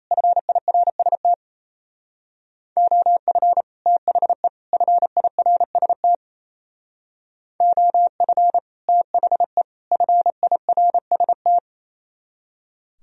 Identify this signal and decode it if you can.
Morse code, heard as a keyed tone, about 36 wpm, 700 Hz. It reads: FIRST OF THE FIRST OF THE FIRST